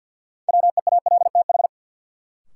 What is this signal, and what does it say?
Morse code, heard as a keyed tone, about 51 wpm, 700 Hz. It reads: WEALTH
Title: Sentences from Top 1000 Words - Review Part 4 - New Word Drill - 50wpm